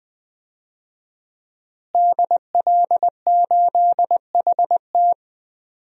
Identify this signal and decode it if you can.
Morse code, heard as a keyed tone, about 20 wpm, 700 Hz. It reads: DL8HT
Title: DXCC-100 Call Signs - 5 Characters - Repeated 3x - 20wpm